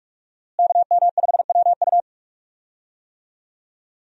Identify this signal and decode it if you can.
Morse code, heard as a keyed tone, about 46 wpm, 700 Hz. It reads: KM5WU
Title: USA Call Signs - 5 Characters - Repeated 2x - 45wpm